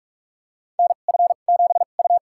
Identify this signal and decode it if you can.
Morse code, heard as a keyed tone, about 46 wpm, spaced as for 33 wpm, 700 Hz. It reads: NF7U
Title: USA Call Signs - 4 Characters - 2x Character Spacing - 45wpm